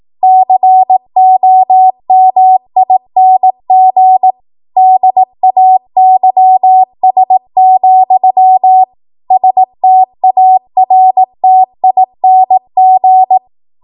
Morse code, heard as a keyed tone, about 18 wpm, 750 Hz. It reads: COMING DAYS, STARTING
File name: Morse / 18WPM